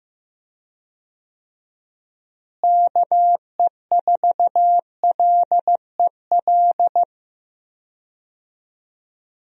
Morse code, heard as a keyed tone, about 16 wpm, 700 Hz. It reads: KE4LEL
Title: USA Call Signs - 6 Characters - Repeated 3x - 15wpm